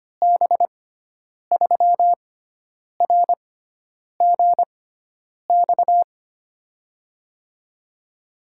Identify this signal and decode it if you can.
Morse code, heard as a keyed tone, about 25 wpm, spaced as for 9 wpm, 700 Hz. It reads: B3RGX
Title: USA Call Signs - 6 Characters - 6x Character Spacing - 25wpm